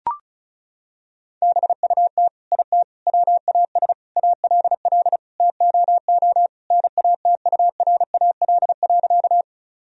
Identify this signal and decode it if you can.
Morse code, heard as a keyed tone, about 36 wpm, 700 Hz. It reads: BUT IT WAS ALL TOO NATURAL.
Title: Sentences from Top 1000 Words - Review Part 4 - 35wpm